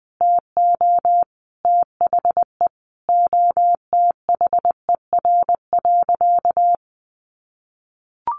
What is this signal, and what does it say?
Morse code, heard as a keyed tone, about 19 wpm, 700 Hz. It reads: TO THE OTHER.
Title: Sentences from Top 1300 Words 20wpm